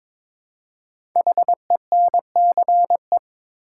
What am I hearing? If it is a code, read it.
Morse code, heard as a keyed tone, about 22 wpm, 690 Hz. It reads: HENCE
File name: 5 Letter Words - First 3 Letters Given - 22wpm